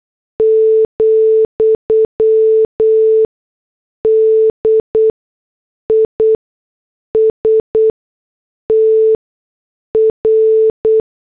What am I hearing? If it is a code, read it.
Morse code, heard as a keyed tone, about 8 wpm, spaced as for 6 wpm, 440 Hz. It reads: , DISTR